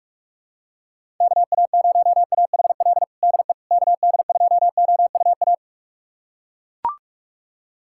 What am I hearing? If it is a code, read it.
Morse code, heard as a keyed tone, about 42 wpm, 700 Hz. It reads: KA0AHL DE KD2OUA
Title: USA Callsign DE Callsign 45wpm